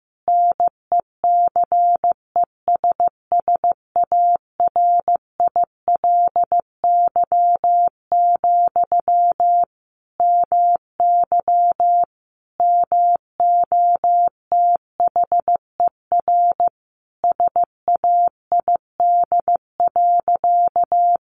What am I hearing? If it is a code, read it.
Morse code, heard as a keyed tone, about 15 wpm, 700 Hz. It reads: NECESSARILY, MY MOTHER SAID.
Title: Sentences from Top 1400 Words 15wpm